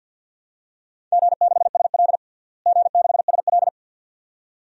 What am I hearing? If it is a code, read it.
Morse code, heard as a keyed tone, about 52 wpm, 700 Hz. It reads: G6SL G6SL
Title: DXCC-100 Call Signs - 4 Characters - Repeated 2x - 50wpm